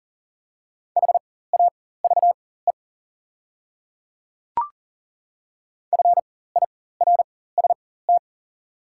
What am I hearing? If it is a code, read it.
Morse code, heard as a keyed tone, about 40 wpm, spaced as for 19 wpm, 690 Hz. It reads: HAVE FIRST
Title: 4x Character Spacing - Top 200 Words - 40wpm